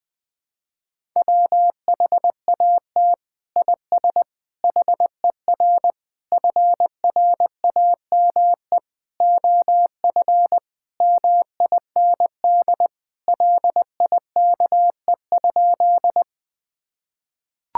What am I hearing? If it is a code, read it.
Morse code, heard as a keyed tone, about 20 wpm, 700 Hz. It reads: WHAT IS HER FRAME OF MIND LIKE?
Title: Sentences from Top 1000 Words - Review Part 6 - 20wpm